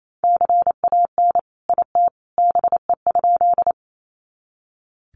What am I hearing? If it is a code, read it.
Morse code, heard as a keyed tone, about 28 wpm, 700 Hz. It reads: CAN IT BE?
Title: Sentences from Top 100 Words 28wpm